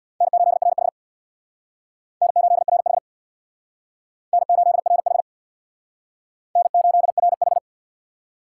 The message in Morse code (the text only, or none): N8RH N8RH N8RH N8RH